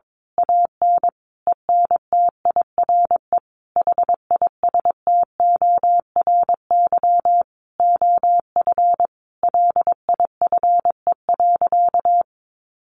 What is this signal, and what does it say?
Morse code, heard as a keyed tone, about 22 wpm, 700 Hz. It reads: AN ENTIRE HISTORY OF LIFE.
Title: A Sound Of Thunder 22wpm